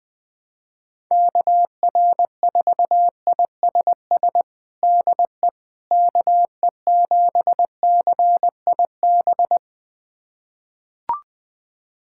Morse code, heard as a keyed tone, about 21 wpm, 700 Hz. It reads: KR4ISS DE KE7CIB
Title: USA Callsign DE Callsign - 6 Characters - 20wpm